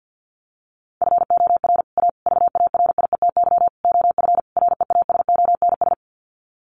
Morse code, heard as a keyed tone, about 50 wpm, 700 Hz. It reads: FOR A VARIETY OF REASONS
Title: Sets of 5 Words Encore 50wpm